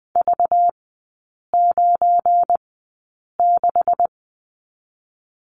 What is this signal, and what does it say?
Morse code, heard as a keyed tone, about 20 wpm, spaced as for 8 wpm, 690 Hz. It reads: V96
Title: 205 Lesson 33e - Letter j - Three Characters 20wpm